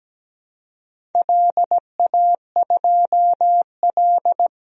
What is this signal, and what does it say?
Morse code, heard as a keyed tone, about 17 wpm, 700 Hz. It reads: LA2L
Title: DXCC-100 Call Signs - 4 Characters - 17wpm